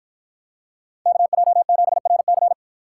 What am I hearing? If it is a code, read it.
Morse code, heard as a keyed tone, about 53 wpm, 700 Hz. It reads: KJ7RC